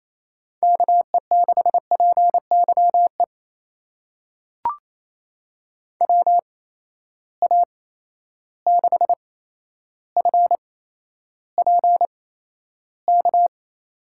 Morse code, heard as a keyed tone, about 27 wpm, spaced as for 12 wpm, 700 Hz. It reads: KE6PYE WA6FPK
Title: USA Call Signs - 6 Characters - 8x Character Spacing - 28wpm